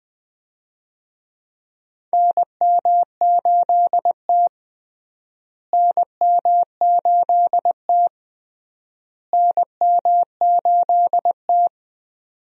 Morse code, heard as a keyed tone, about 20 wpm, 700 Hz. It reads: NM8T NM8T NM8T
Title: USA Call Signs - 4 Characters - Repeated 3x - 20wpm